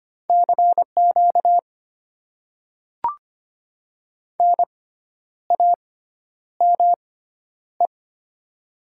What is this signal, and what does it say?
Morse code, heard as a keyed tone, about 25 wpm, spaced as for 10 wpm, 700 Hz. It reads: CQ NAME
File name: Top 90 QSO Elements - 6x Character Spacing - 25wpm